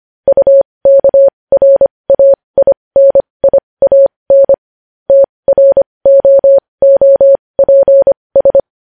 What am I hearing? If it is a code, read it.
Morse code, heard as a keyed tone, about 25 wpm, spaced as for 20 wpm, 550 Hz. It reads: UKRAINIAN TROOPS
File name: podcast / News headlines-1x-25x20